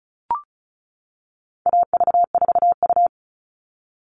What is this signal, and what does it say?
Morse code, heard as a keyed tone, about 34 wpm, 700 Hz. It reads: AV4U
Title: All Cut Numbers - 4-Digit 35wpm